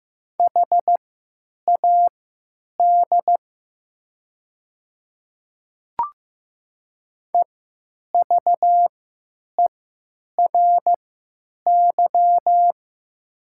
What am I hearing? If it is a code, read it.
Morse code, heard as a keyed tone, about 15 wpm, spaced as for 9 wpm, 700 Hz. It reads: HAD EVERY
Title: 3x Character Spacing - Top 200 Words - 15wpm